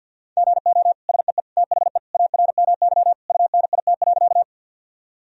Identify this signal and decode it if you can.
Morse code, heard as a keyed tone, about 50 wpm, 700 Hz. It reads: GO SEE THE ARMY UNIT.
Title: Sentences from Top 1000 Words - Review Part 2 - 50wpm